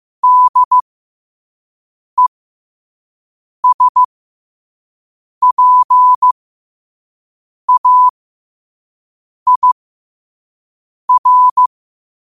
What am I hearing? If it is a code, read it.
Morse code, heard as a keyed tone, about 15 wpm, spaced as for 5 wpm, 1000 Hz. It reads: DESPAIR
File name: qotdfiles / 5wpm